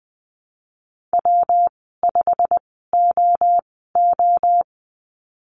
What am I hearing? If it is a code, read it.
Morse code, heard as a keyed tone, about 20 wpm, spaced as for 14 wpm, 700 Hz. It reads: W5OO
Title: 122 Lesson 20g - Number 3 - Call Signs 20wpm